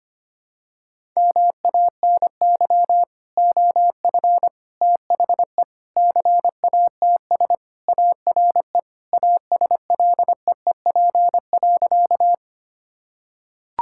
Morse code, heard as a keyed tone, about 25 wpm, 700 Hz. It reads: MANY OF THE CATS ARE ASLEEP.
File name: practice / 25wpm / Sentences with 3-Word Phrases - 25wpm